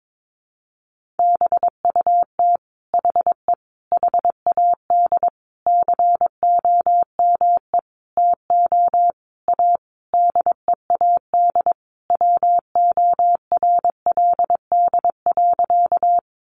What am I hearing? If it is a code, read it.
Morse code, heard as a keyed tone, about 22 wpm, 700 Hz. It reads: BUT HE HAD COME TO A DEAD WORLD.
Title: Sentences from Top 1500 Words 22wpm